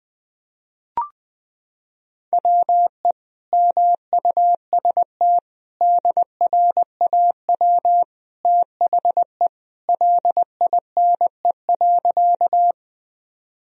Morse code, heard as a keyed tone, about 21 wpm, 700 Hz. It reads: WE MUST DRAW THE LINE.
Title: Sentences from Top 1000 Words - Review Part 1 - 20wpm